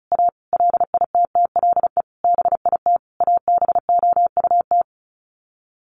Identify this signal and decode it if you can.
Morse code, heard as a keyed tone, about 34 wpm, 700 Hz. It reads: A LITTLE BIT ABOUT
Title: Sets of 5 Words Encore 35wpm